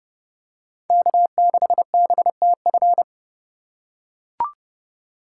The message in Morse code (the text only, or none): K6BTF